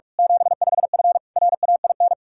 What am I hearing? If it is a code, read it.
Morse code, heard as a keyed tone, about 46 wpm, 700 Hz. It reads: 75F RAIN
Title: Practice QSO - Exchange 2 - Info - 45wpm